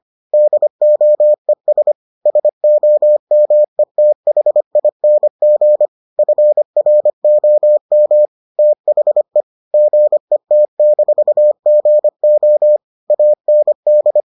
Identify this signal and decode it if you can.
Morse code, heard as a keyed tone, about 25 wpm, 600 Hz. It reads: DOES SOMETHING FROM THE GET-GO AND